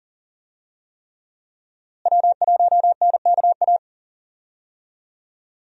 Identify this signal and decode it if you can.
Morse code, heard as a keyed tone, about 38 wpm, 700 Hz. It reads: W1NKA
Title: USA Call Signs - 5 Characters - Repeated 3x - 40wpm